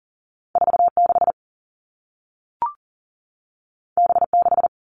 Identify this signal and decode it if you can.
Morse code, heard as a keyed tone, about 38 wpm, 700 Hz. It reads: 46 B6
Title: Two Challenging Letter-Number Combinations 40wpm